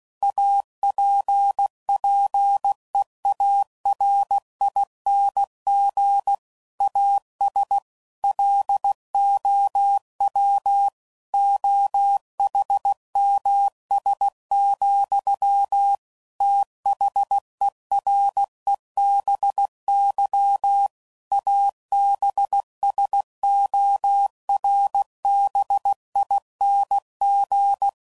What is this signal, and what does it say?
Morse code, heard as a keyed tone, about 16 wpm, 800 Hz. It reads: APPEARING AS LOW OHMS, THEREBY ABSORBING
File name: CW Code Groups @15WPM / Technical Subjects / XC